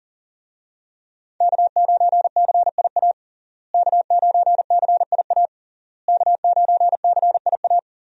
Morse code, heard as a keyed tone, about 40 wpm, 700 Hz. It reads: K9CIA K9CIA K9CIA
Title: USA Call Signs - 5 Characters - Repeated 3x - 40wpm